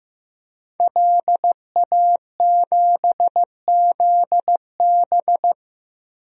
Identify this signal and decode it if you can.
Morse code, heard as a keyed tone, about 15 wpm, 700 Hz. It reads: LA7ZB
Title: DXCC-100 Call Signs - 5 Characters - 15wpm